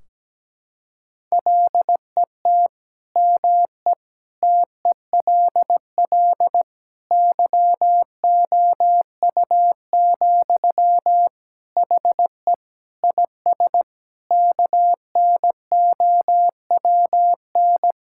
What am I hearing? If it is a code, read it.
Morse code, heard as a keyed tone, about 17 wpm, 700 Hz. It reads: LET ME TELL YOU, HE IS KNOWN
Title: Sentences from Top 1600 Words 17wpm